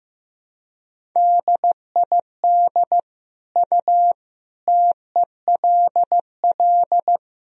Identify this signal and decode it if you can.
Morse code, heard as a keyed tone, about 15 wpm, 700 Hz. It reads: DID U TELL
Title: Sentences from Top 200 Words and Abbreviations 15wpm